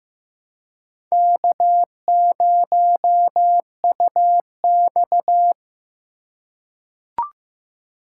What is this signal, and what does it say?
Morse code, heard as a keyed tone, about 15 wpm, 700 Hz. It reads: K0UX
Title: USA Call Signs - 3x Character Spacing - 15wpm